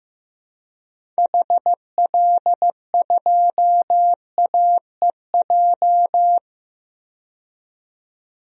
Morse code, heard as a keyed tone, about 16 wpm, 700 Hz. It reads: HL2AEJ
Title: DXCC-100 Call Signs - 6 Characters - 15wpm